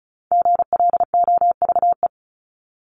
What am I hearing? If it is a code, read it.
Morse code, heard as a keyed tone, about 34 wpm, 700 Hz. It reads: GLOVE
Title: Top 1751-2000 Words 35wpm